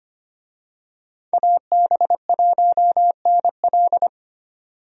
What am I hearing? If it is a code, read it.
Morse code, heard as a keyed tone, about 25 wpm, 700 Hz. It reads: AB1NL
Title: USA Call Signs - 5 Characters - 25wpm